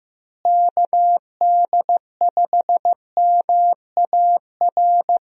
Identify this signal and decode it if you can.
Morse code, heard as a keyed tone, about 15 wpm, 700 Hz. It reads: KD5MAR
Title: USA Call Signs - 6 Characters - Repeated 3x - 15wpm